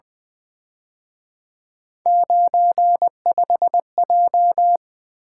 Morse code, heard as a keyed tone, about 23 wpm, 700 Hz. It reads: 95J
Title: Speed-Racing - Three Letter-Number Combinations - 20wpm